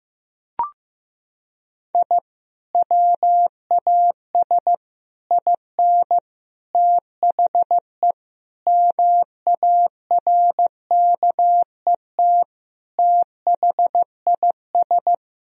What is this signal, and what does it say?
Morse code, heard as a keyed tone, about 16 wpm, 700 Hz. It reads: I WAS IN THE MARKET THIS
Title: Sentences from Top 1000 Words - Review Part 3 - 15wpm